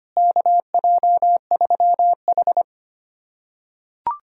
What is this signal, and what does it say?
Morse code, heard as a keyed tone, about 25 wpm, 700 Hz. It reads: KJ3H
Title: USA Callsign DE Callsign - 4 Characters - 25wpm